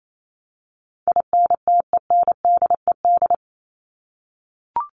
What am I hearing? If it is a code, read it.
Morse code, heard as a keyed tone, about 28 wpm, 700 Hz. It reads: INTENDED
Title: ED Suffix - Top 100 Words with Base Words 28wpm